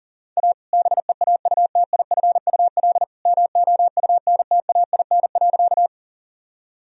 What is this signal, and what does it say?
Morse code, heard as a keyed tone, about 40 wpm, 700 Hz. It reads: A BEAUTIFUL MOUNTAIN.
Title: Sentences from Top 1000 Words - Review Part 1 - 40wpm